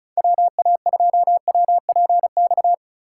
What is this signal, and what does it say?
Morse code, heard as a keyed tone, about 44 wpm, 700 Hz. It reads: WA2WPX
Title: Speed Racing - USA Call Signs 35wpm